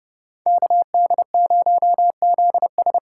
Morse code, heard as a keyed tone, about 37 wpm, 700 Hz. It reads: KD0ZS
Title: Speed Racing - USA 2x2 Call Signs 30wpm